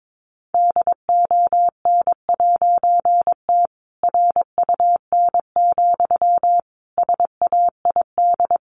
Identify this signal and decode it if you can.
Morse code, heard as a keyed tone, about 22 wpm, 700 Hz. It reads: DON'T RUN, SAID
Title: A Sound Of Thunder 22wpm